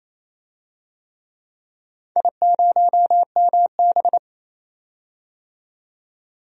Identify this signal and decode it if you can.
Morse code, heard as a keyed tone, about 28 wpm, 700 Hz. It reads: I0MB
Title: DXCC-100 Call Signs - 4 Characters - 28wpm